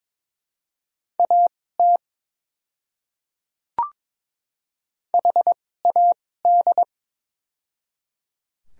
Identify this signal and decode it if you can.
Morse code, heard as a keyed tone, about 23 wpm, spaced as for 16 wpm, 700 Hz. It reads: AT HAD
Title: 2x Character Spacing - Top 200 Words - 22wpm